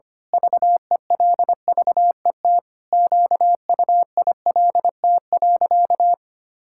Morse code, heard as a keyed tone, about 25 wpm, 700 Hz. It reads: VELVET QUILT.
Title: Pangram Challenge v2 - 25wpm